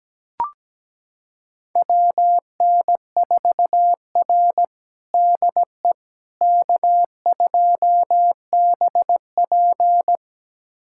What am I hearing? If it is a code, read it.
Morse code, heard as a keyed tone, about 17 wpm, 700 Hz. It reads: WN4R DE K2BP
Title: USA Callsign DE Callsign - 4 Characters - 17wpm